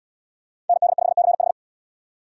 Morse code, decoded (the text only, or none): ND5CH